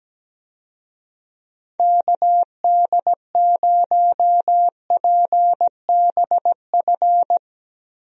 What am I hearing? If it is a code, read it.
Morse code, heard as a keyed tone, about 17 wpm, 700 Hz. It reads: KD0PBF
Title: USA Call Signs - 6 Characters - Repeated 3x - 17wpm